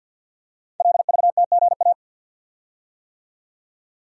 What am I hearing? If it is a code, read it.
Morse code, heard as a keyed tone, about 50 wpm, 700 Hz. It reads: LVTGA
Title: Code Groups - 5 Letters - 50wpm